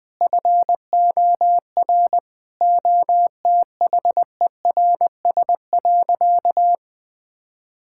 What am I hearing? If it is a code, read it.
Morse code, heard as a keyed tone, about 20 wpm, 700 Hz. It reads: FOR OTHERS.
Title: Sentences with 6-Word Phrases - 20wpm